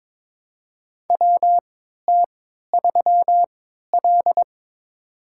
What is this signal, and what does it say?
Morse code, heard as a keyed tone, about 22 wpm, spaced as for 12 wpm, 700 Hz. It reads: WT3L
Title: USA Call Signs - 4 Characters - 3x Character Spacing - 22wpm